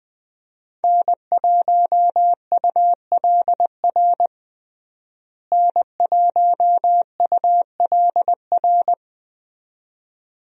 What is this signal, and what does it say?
Morse code, heard as a keyed tone, about 21 wpm, 700 Hz. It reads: N1ULR N1ULR
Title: USA Call Signs - 5 Characters - Repeated 2x - 20wpm